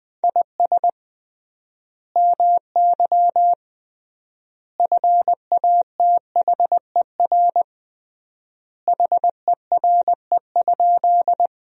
Morse code, heard as a keyed tone, about 20 wpm, 700 Hz. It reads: IS MY FATHER HERE?
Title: 3x Word-Spacing - Sentences from Top 300 Words - 20wpm